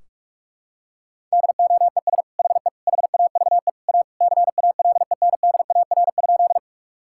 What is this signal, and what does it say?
Morse code, heard as a keyed tone, about 42 wpm, 700 Hz. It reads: DOES HE HAVE A CALENDAR?
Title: Sentences from Top 1600 Words 45wpm